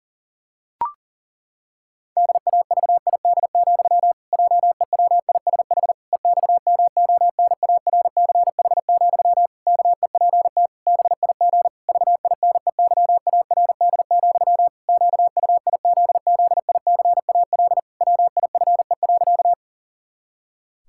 Morse code, heard as a keyed tone, about 37 wpm, 700 Hz. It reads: DAVID, JEWISH EXMONARCH, KEPT BIG VINEYARD, QUIZZICAL WIFE.
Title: Pangram Challenge v2 - 40wpm